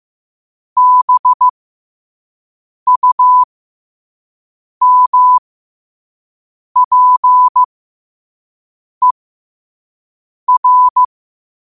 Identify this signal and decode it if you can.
Morse code, heard as a keyed tone, about 15 wpm, spaced as for 5 wpm, 1000 Hz. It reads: BUMPER